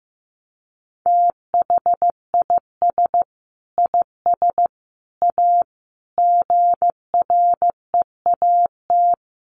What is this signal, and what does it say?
Morse code, heard as a keyed tone, about 15 wpm, 700 Hz. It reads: THIS IS A GREAT
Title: Sentences from Top 1300 Words 15wpm